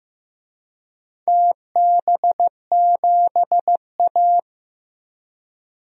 Morse code, heard as a keyed tone, about 15 wpm, 700 Hz. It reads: TB7A
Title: DXCC-100 Call Signs - 4 Characters - 15wpm